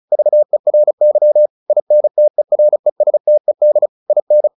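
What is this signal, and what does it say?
Morse code, heard as a keyed tone, about 34 wpm, 600 Hz. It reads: VERY INTERESTED IN
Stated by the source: Morse practice at 35 wpm from CW med Gnister, sending Ken Follett's novel Never in English.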